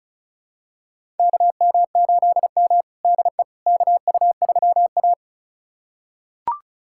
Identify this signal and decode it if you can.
Morse code, heard as a keyed tone, about 36 wpm, 700 Hz. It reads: KM8M DE KU3A
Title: USA Callsign DE Callsign - 4 Characters - 35wpm